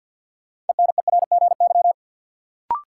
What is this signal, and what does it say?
Morse code, heard as a keyed tone, about 53 wpm, 700 Hz. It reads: ENERGY
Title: Sentences from Top 900 Words - New Word Drill - 50wpm